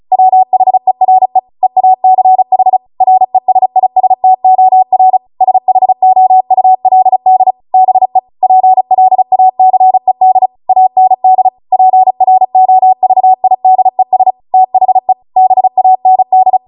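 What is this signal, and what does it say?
Morse code, heard as a keyed tone, about 36 wpm, 750 Hz. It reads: WHERE EACH RESISTOR SHOULD BE PLACED AND PROVIDES THE BAND
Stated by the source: Amateur radio Morse code practice archive transmission at 35 wpm.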